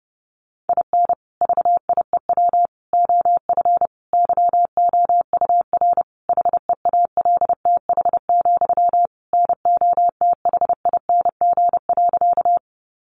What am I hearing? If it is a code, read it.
Morse code, heard as a keyed tone, about 30 wpm, 700 Hz. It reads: IN VIEW OF YOUR HEALTH, NOTHING.
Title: Sentences from Top 900 Words 30wpm